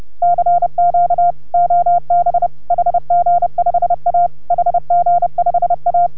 Morse code, heard as a keyed tone, about 30 wpm, 690 Hz. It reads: CQ OB HG5A HG5A